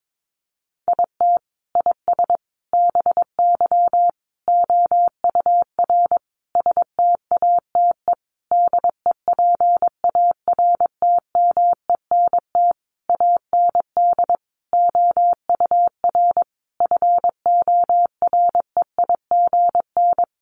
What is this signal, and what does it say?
Morse code, heard as a keyed tone, about 22 wpm, 700 Hz. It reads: IT IS BY OUR STATE DEPARTMENT AND OUR FOREIGN
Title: Sentences from Top 1400 Words 22wpm